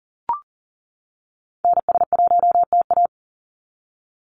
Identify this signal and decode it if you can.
Morse code, heard as a keyed tone, about 38 wpm, 700 Hz. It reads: NS1TA
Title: NA SOTA Activator Callsigns 40wpm